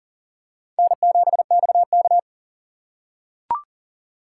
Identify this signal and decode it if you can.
Morse code, heard as a keyed tone, about 40 wpm, 700 Hz. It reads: N7XK
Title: USA Call Signs - 6x Character Spacing - 40wpm